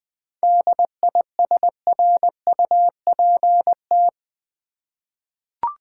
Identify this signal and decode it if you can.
Morse code, heard as a keyed tone, about 21 wpm, 700 Hz. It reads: DISRUPT
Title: 7 Letter Words - First 3 Letters Given - 20wpm